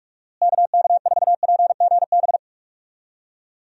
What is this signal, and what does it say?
Morse code, heard as a keyed tone, about 41 wpm, 700 Hz. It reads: KK4PGB
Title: USA Call Signs - 6 Characters - Repeated 2x - 45wpm